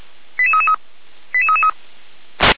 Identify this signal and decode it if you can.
Five-tone selective call (ZVEI), heard as quick runs of five tones, tones of about 70 ms. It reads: 88333 88333